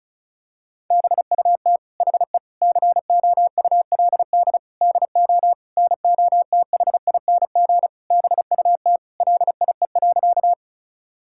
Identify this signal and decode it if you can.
Morse code, heard as a keyed tone, about 36 wpm, 700 Hz. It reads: BUT HE COULD DO NOTHING BUT LIE.
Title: Sentences from Top 1600 Words 35wpm